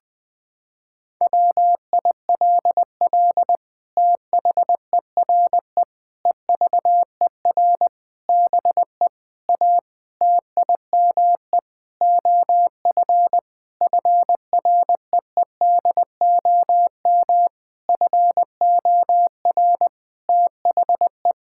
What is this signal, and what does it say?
Morse code, heard as a keyed tone, about 20 wpm, 700 Hz. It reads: WILL THERE EVER BE A TIME OF FREEDOM FOR THE
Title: Sentences from Top 1000 Words - Review Part 5 - 20wpm